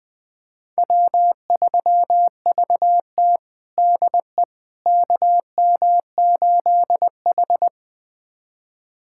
Morse code, heard as a keyed tone, about 20 wpm, 700 Hz. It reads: W3VT DE KM8H
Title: USA Callsign DE Callsign - 4 Characters - 20wpm